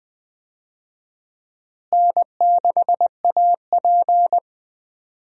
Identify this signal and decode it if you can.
Morse code, heard as a keyed tone, about 20 wpm, 700 Hz. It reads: N6AP
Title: USA Call Signs - 4 Characters - Repeated 2x - 20wpm